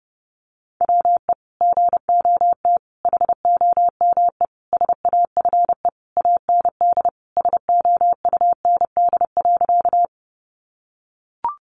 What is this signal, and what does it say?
Morse code, heard as a keyed tone, about 30 wpm, 700 Hz. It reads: WE GOT HOME SAFE AND SOUND.
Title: Sentences from Top 900 Words 30wpm